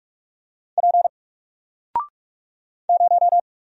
Morse code, heard as a keyed tone, about 44 wpm, 700 Hz. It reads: P 0
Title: Single Character (Letter:Number:Punctuation:Prosign) 45wpm